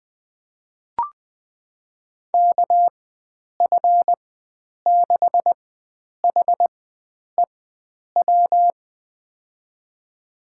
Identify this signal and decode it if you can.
Morse code, heard as a keyed tone, about 20 wpm, spaced as for 9 wpm, 700 Hz. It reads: KF6HEW
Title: USA Call Signs - 4x Character Spacing - 20wpm